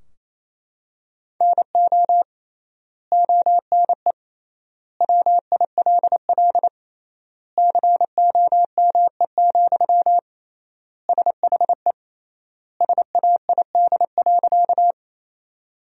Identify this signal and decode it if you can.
Morse code, heard as a keyed tone, about 27 wpm, 700 Hz. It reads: NO ONE WILL COME, SHE SAID.
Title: 3x Word-Spacing - Sentences from Top 100 Words - 28wpm